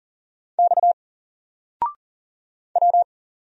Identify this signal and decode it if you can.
Morse code, heard as a keyed tone, about 40 wpm, 700 Hz. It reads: X W
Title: Single Character (Letter:Number:Punctuation:Prosign) Alt 40wpm